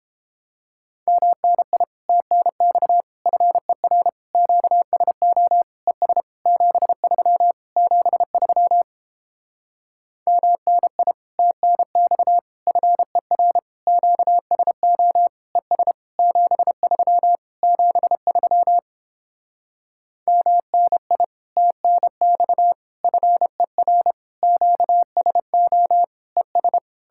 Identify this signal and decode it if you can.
Morse code, heard as a keyed tone, about 29 wpm, 700 Hz. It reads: MNI TNX FER QSO ES 73 73 MNI TNX FER QSO ES 73 73 MNI TNX FER QSO ES